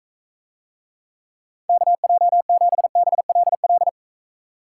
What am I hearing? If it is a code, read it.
Morse code, heard as a keyed tone, about 42 wpm, 700 Hz. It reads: KJ7BLL